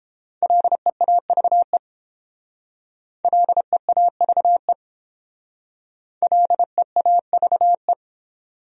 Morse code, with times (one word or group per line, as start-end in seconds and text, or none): 0.4-1.8 LEAVE
3.2-4.7 LEAVE
6.2-7.9 LEAVE